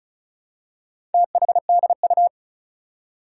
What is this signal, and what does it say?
Morse code, heard as a keyed tone, about 36 wpm, 700 Hz. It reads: THDU
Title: Four Challenging Letter-Number Combinations 35wpm